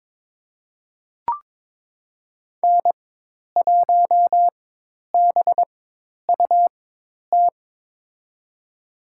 Morse code, heard as a keyed tone, about 22 wpm, spaced as for 10 wpm, 700 Hz. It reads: N1BUT
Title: USA Call Signs - 4x Character Spacing - 22wpm